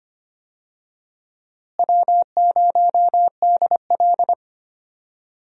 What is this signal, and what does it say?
Morse code, heard as a keyed tone, about 25 wpm, 700 Hz. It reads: W0DL
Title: USA Call Signs - 4 Characters - Repeated 2x - 25wpm